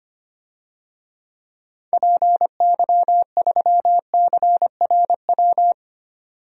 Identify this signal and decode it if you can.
Morse code, heard as a keyed tone, about 25 wpm, 690 Hz. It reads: PY3CRW
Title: DXCC-100 Call Signs - 6 Characters - 25wpm